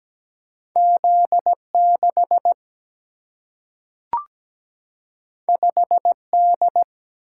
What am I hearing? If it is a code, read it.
Morse code, heard as a keyed tone, about 17 wpm, 700 Hz. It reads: Z6 5D
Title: Single-Letter Single-Number 17wpm